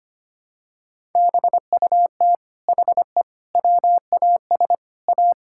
Morse code, heard as a keyed tone, about 25 wpm, 700 Hz. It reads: BUT HE WAS A
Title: Sentences from Top 1000 Words - Review Part 6 - 25wpm